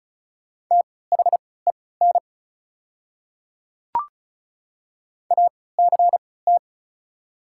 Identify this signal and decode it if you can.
Morse code, heard as a keyed tone, about 34 wpm, spaced as for 20 wpm, 700 Hz. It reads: THEN ACT
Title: 3x Character Spacing - Top 300 Words - 35wpm